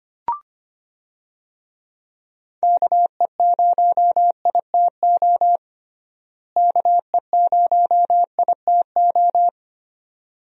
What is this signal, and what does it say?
Morse code, heard as a keyed tone, about 25 wpm, 700 Hz. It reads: KE0ITO KE0ITO
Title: USA Call Signs - 6 Characters - Repeated 2x - 25wpm